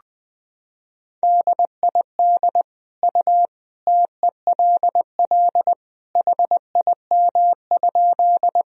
Morse code, heard as a keyed tone, about 20 wpm, 700 Hz. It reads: DID U TELL HIM?
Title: Sentences from Top 200 Words and Abbreviations 20wpm